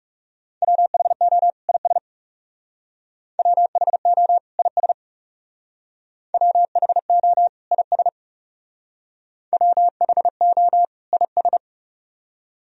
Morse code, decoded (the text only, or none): WHO IS WHO IS WHO IS WHO IS